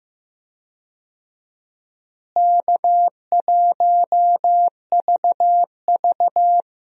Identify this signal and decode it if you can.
Morse code, heard as a keyed tone, about 15 wpm, 700 Hz. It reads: K1VV
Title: USA Call Signs - 4 Characters - Repeated 3x - 15wpm